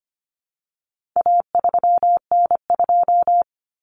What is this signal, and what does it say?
Morse code, heard as a keyed tone, about 25 wpm, 700 Hz. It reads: A3N2
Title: Common Cut Numbers - 4-Digit 25wpm